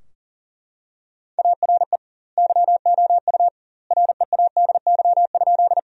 Morse code, heard as a keyed tone, about 40 wpm, 700 Hz. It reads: ARE YOU READY?
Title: 2x Word-Spacing - Sentences from Top 400 Words - 40wpm